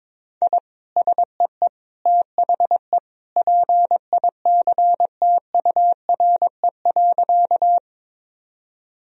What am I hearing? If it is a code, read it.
Morse code, heard as a keyed tone, about 22 wpm, 700 Hz. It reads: I SEE THE PICTURE.
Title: Sentences from Top 1000 Words - Review Part 1 - 22wpm